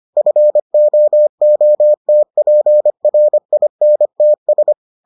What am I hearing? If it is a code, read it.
Morse code, heard as a keyed tone, about 25 wpm, 600 Hz. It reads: FOOTPRINTS